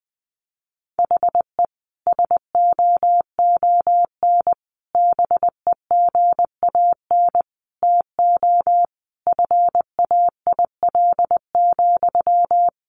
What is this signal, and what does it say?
Morse code, heard as a keyed tone, about 19 wpm, 700 Hz. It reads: HE SOON BEGAN TO FAIL,
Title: Sentences from Top 1100 Words 20wpm